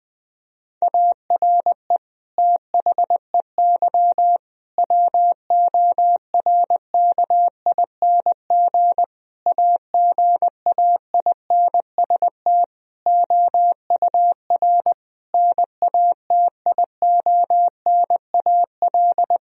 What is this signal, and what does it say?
Morse code, heard as a keyed tone, about 20 wpm, 700 Hz. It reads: ARE THEY WORKING AGAINST OUR NATIONAL
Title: Sentences from Top 1000 Words - Review Part 4 - 20wpm